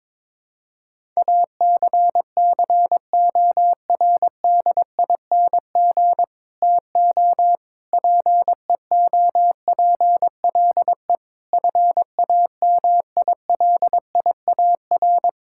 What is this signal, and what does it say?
Morse code, heard as a keyed tone, about 22 wpm, 700 Hz. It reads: ACCORDING TO PEOPLE FAMILIAR
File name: practice / 22wpm / Sets of 7 Words - 22wpm